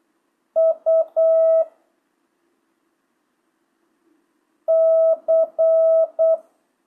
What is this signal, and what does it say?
Morse code, heard as a keyed tone, about 8 wpm, 650 Hz. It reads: U C
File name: jh8cbh / 2025cw